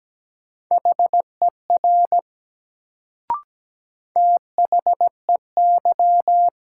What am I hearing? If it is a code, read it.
Morse code, heard as a keyed tone, about 17 wpm, 700 Hz. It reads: HER THEY
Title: Sentences from Top 100 Words - New Word Drill - 17wpm